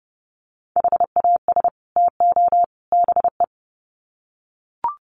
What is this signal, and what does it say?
Morse code, heard as a keyed tone, about 30 wpm, 700 Hz. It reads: HAS TO BE
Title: Sets of 3 Words 30wpm